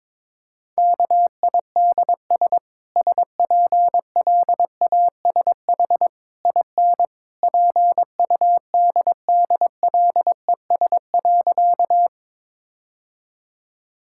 Morse code, heard as a keyed tone, about 22 wpm, 700 Hz. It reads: KIDS SPLASH IN PUDDLES.
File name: practice / 22wpm / Easy Sentences with Hints - 22wpm